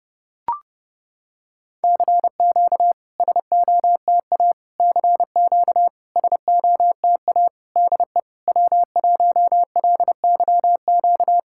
Morse code, heard as a keyed tone, about 30 wpm, 700 Hz. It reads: CQ SOTA CQ SOTA DE W1LYQ
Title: SOTA QSO Practice 30wpm